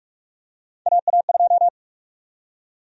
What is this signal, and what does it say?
Morse code, heard as a keyed tone, about 41 wpm, 700 Hz. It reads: AA2
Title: USA Call Sign Prefixes - Repeated 2x - 45wpm